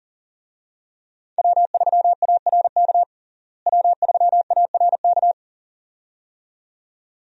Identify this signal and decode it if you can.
Morse code, heard as a keyed tone, about 37 wpm, 700 Hz. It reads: W3ARK W3ARK
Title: USA Call Signs - 5 Characters - Repeated 2x - 40wpm